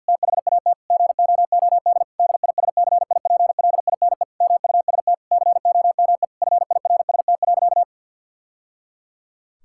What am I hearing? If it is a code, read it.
Morse code, heard as a keyed tone, about 50 wpm, 700 Hz. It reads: THAT GOOD DISCIPLINE MUST COME FIRST.
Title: Sentences from Top 1000 Words - Review Part 6 - 50wpm